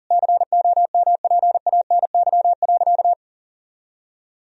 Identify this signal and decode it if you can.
Morse code, heard as a keyed tone, about 40 wpm, 700 Hz. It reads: COMPANY.
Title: Sentences from Top 1000 Words - Review Part 3 - 40wpm